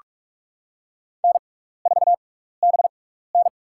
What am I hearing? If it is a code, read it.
Morse code, heard as a keyed tone, about 44 wpm, spaced as for 15 wpm, 700 Hz. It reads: N4BN
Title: USA Call Signs - 6x Character Spacing - 45wpm